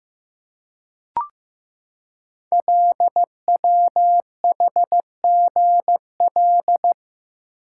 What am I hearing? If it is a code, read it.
Morse code, heard as a keyed tone, about 15 wpm, 700 Hz. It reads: LWHGL
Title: Code Groups - 5 Letter:Numbers - 15wpm